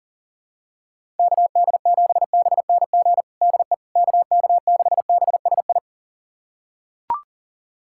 Morse code, heard as a keyed tone, about 37 wpm, 700 Hz. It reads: KD7BNG DE KK6BSI
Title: USA Callsign DE Callsign - 6 Characters - 40wpm